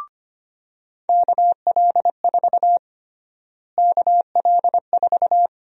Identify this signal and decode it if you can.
Morse code, heard as a keyed tone, about 25 wpm, 700 Hz. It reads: KL4 KL4
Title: USA Call Sign Prefixes - Repeated 2x - 25wpm